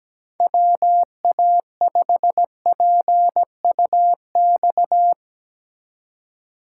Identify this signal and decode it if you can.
Morse code, heard as a keyed tone, about 18 wpm, 700 Hz. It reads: WA5PUX
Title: Speed Racing - USA 2x3 Call Signs 17wpm